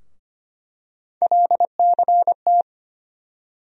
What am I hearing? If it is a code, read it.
Morse code, heard as a keyed tone, about 25 wpm, 700 Hz. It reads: LCT
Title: USA Call Sign Postfixes - 3 Letters 25wpm